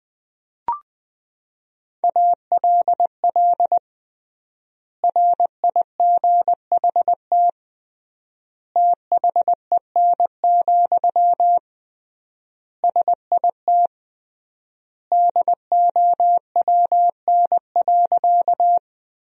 Morse code, heard as a keyed tone, about 21 wpm, 700 Hz. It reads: ALL RIGHT THEN, SIT DOWN.
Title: 3x Word-Spacing - Sentences from Top 500 Words - 20wpm